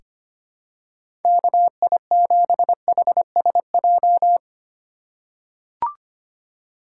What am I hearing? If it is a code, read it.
Morse code, heard as a keyed tone, about 25 wpm, 700 Hz. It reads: KI7HSJ
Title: USA Call Signs - 4x Character Spacing - 25wpm